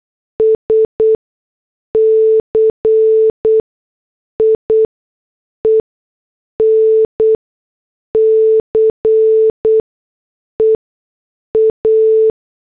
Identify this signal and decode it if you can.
Morse code, heard as a keyed tone, about 8 wpm, spaced as for 6 wpm, 440 Hz. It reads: SCIENCE A